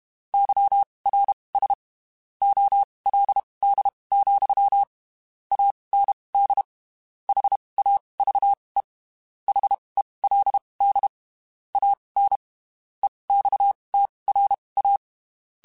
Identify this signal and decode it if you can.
Morse code, heard as a keyed tone, about 32 wpm, spaced as for 23 wpm, 800 Hz. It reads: YRS OLD, AND HAVE HELD AN EXTRA
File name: (spaced) / QSOs Spaced-mp3s / kc7haa @30 sp